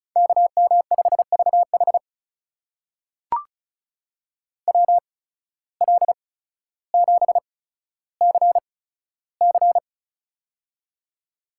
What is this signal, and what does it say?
Morse code, heard as a keyed tone, about 34 wpm, spaced as for 15 wpm, 700 Hz. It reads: KM5VH WL7CC
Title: USA Call Signs - 5 Characters - 8x Character Spacing - 35wpm